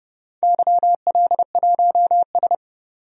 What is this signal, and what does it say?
Morse code, heard as a keyed tone, about 32 wpm, 700 Hz. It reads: YL1S
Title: DXCC-100 Call Signs - 4 Characters - 30wpm